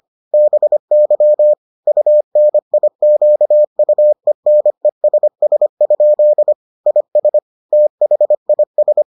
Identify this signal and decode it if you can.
Morse code, heard as a keyed tone, about 25 wpm, 600 Hz. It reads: BY UNIQUENESS? IS THIS